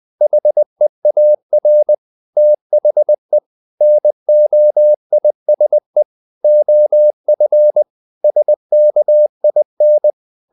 Morse code, heard as a keyed tone, about 20 wpm, 600 Hz. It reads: HEAR THE NOISE OF SKIN
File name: the old man and the sea part 2 20wpm